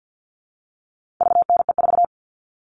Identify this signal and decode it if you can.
Morse code, heard as a keyed tone, about 50 wpm, 700 Hz. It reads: VNE4
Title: All Cut Numbers - 4-Digit 50wpm